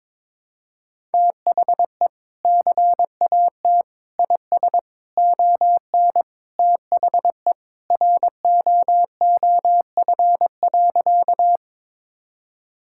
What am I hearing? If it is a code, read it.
Morse code, heard as a keyed tone, about 22 wpm, 700 Hz. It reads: THE CAT IS ON THE ROOF.
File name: practice / 22wpm / Sentences with 3-Word Phrases - 22wpm